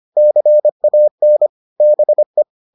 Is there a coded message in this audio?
Yes: Morse code from 0.2 to 2.4 s, heard as a keyed tone, about 25 wpm, 600 Hz. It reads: CAN BE